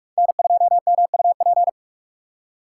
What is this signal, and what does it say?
Morse code, heard as a keyed tone, about 43 wpm, 700 Hz. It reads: N2MUP